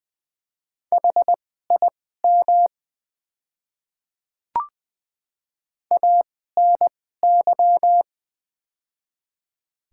Morse code, heard as a keyed tone, about 20 wpm, spaced as for 15 wpm, 700 Hz. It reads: HIM ANY
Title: 2x Character Spacing - Top 200 Words - 20wpm